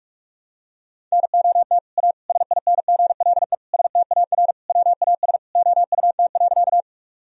Morse code, heard as a keyed tone, about 44 wpm, 700 Hz. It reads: NOT A SINGLE STAR WAS OUT.
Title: Sentences from Top 1000 Words - Review Part 4 - 45wpm